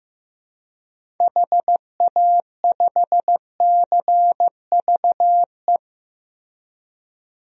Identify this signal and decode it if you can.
Morse code, heard as a keyed tone, about 15 wpm, 700 Hz. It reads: HA5CVE
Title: DXCC-100 Call Signs - 6 Characters - Repeated 2x - 15wpm